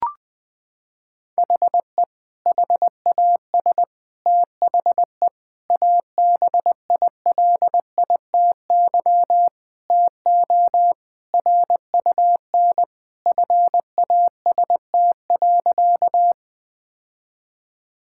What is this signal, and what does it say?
Morse code, heard as a keyed tone, about 20 wpm, 700 Hz. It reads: HE HAS THE ABILITY TO RUN FAST.
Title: Sentences with 3-Word Phrases - 20wpm